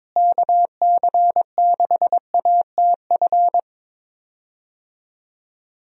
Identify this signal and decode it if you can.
Morse code, heard as a keyed tone, about 21 wpm, 700 Hz. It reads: KC6ATF
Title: USA Call Signs - 6 Characters - Repeated 3x - 22wpm